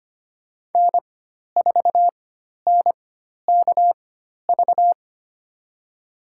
Morse code, heard as a keyed tone, about 25 wpm, spaced as for 12 wpm, 700 Hz. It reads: N4NKV
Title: USA Call Signs - 4x Character Spacing - 25wpm